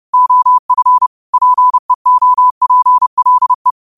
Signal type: Morse code